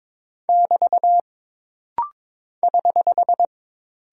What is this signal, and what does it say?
Morse code, heard as a keyed tone, about 22 wpm, 700 Hz. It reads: <BT> <HH>
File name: practice / 22wpm / Single Punctuation-Prosign - Rapid-Fire - 22wpm